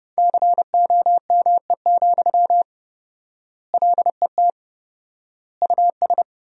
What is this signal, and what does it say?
Morse code, heard as a keyed tone, about 30 wpm, 700 Hz. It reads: COME, LET US
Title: 4x Word-Spacing - Sentences from Top 300 Words - 30wpm